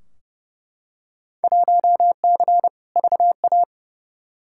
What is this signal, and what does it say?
Morse code, heard as a keyed tone, about 31 wpm, 700 Hz. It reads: 1C VA
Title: Field Day Exchanges Encore - 30wpm